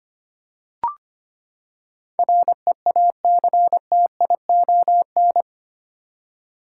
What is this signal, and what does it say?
Morse code, heard as a keyed tone, about 25 wpm, 700 Hz. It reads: REACTION
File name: Eight Letter Words - 25wpm